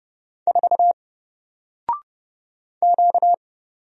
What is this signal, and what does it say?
Morse code, heard as a keyed tone, about 30 wpm, 700 Hz. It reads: 4 Q
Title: Single Character (Letter:Number:Punctuation:Prosign) Alt 30wpm